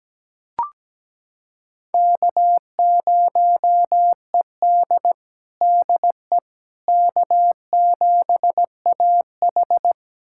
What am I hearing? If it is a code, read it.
Morse code, heard as a keyed tone, about 17 wpm, 700 Hz. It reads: K0ED DE K7AH
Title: USA Callsign DE Callsign - 4 Characters - 17wpm